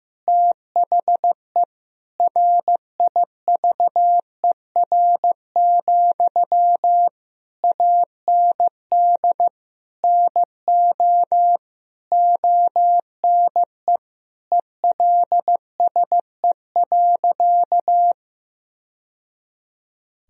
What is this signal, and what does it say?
Morse code, heard as a keyed tone, about 15 wpm, 700 Hz. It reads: THE RIVER, AND NO ONE ELSE.
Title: Sentences from Top 1000 Words - Review Part 1 - 15wpm